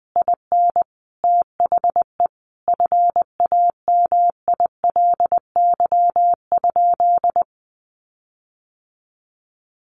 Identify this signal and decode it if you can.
Morse code, heard as a keyed tone, about 20 wpm, 700 Hz. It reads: IN THE FAMILY?
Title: Sentences from Top 1400 Words 20wpm